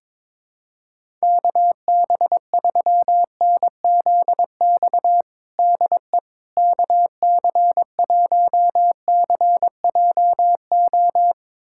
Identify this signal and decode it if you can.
Morse code, heard as a keyed tone, about 22 wpm, 700 Hz. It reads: KB3NZX DE KC1CJO
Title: USA Callsign DE Callsign 22wpm